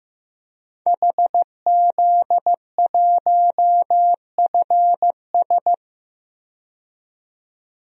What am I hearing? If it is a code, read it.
Morse code, heard as a keyed tone, about 15 wpm, 700 Hz. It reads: HZ1FS
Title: DXCC-100 Call Signs - 5 Characters - 15wpm